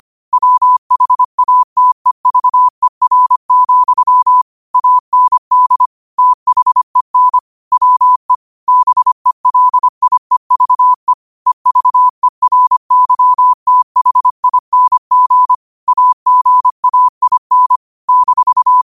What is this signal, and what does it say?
Morse code, heard as a keyed tone, about 25 wpm, 1000 Hz. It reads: WHATEVER, AND THEN WE BELIEVE EVERYTHING AGAIN -